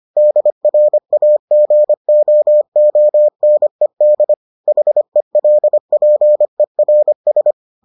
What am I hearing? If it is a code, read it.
Morse code, heard as a keyed tone, about 25 wpm, 600 Hz. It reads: DRAGOONED HELPERS